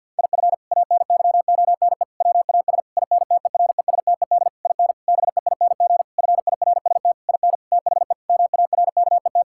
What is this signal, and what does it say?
Morse code, heard as a keyed tone, about 50 wpm, 700 Hz. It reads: IF ANYONE WAS INTERESTED IN BEING FIRST IN THE MARKET
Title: Sentences from Top 1600 Words 50wpm